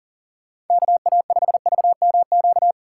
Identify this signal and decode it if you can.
Morse code, heard as a keyed tone, about 50 wpm, 700 Hz. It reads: KA5VMQ